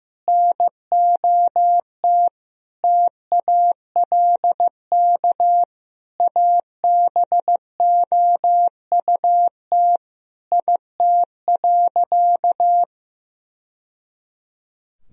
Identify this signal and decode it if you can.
Morse code, heard as a keyed tone, about 16 wpm, 700 Hz. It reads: NOT TALK ABOUT IT.
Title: Sentences from Top 1000 Words - Review Part 4 - 15wpm